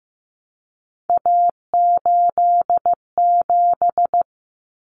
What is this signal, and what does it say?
Morse code, heard as a keyed tone, about 15 wpm, 700 Hz. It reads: A87
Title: Common Cut Numbers - 3-Digit 15wpm